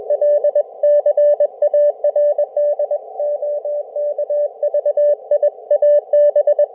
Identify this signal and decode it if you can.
Morse code, heard as a keyed tone, about 21 wpm, 590 Hz, with radio noise. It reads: L CARD OK VIA B